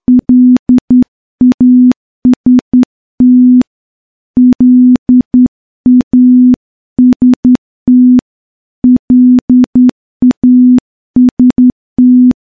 Morse code, heard as a keyed tone, about 10 wpm, 250 Hz. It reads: LAST LAST LAST
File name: morse / morse-training-10WPM